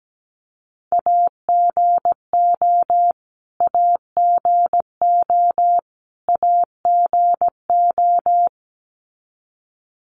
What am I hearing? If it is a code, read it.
Morse code, heard as a keyed tone, about 17 wpm, 700 Hz. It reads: AGO AGO AGO
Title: Top 500 Words Repeated 17wpm